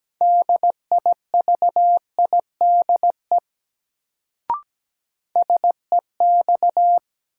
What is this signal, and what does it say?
Morse code, heard as a keyed tone, about 17 wpm, 700 Hz. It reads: DIVIDE SEX
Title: Sentences from Top 1600 Words - New Word Drill - 17wpm